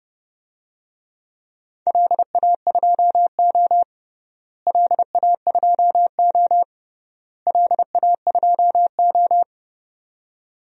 Morse code, heard as a keyed tone, about 30 wpm, 700 Hz. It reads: LA2O LA2O LA2O
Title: DXCC-100 Call Signs - 4 Characters - Repeated 3x - 30wpm